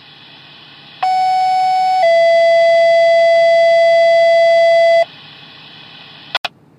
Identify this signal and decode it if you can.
Two-tone page, heard as a one-second tone, then a three-second tone, about 750 then 670 Hz. It is heard from 1.0 to 5.0 s.